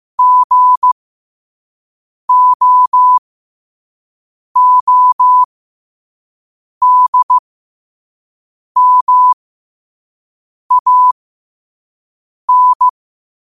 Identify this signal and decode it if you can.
Morse code, heard as a keyed tone, about 15 wpm, spaced as for 5 wpm, 1000 Hz. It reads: GOODMAN
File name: qotdfiles / 5wpm